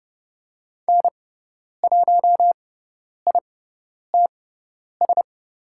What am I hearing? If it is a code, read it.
Morse code, heard as a keyed tone, about 30 wpm, spaced as for 10 wpm, 700 Hz. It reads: N1ITS
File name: Lesson 17g - Letter u - Call Signs 30wpm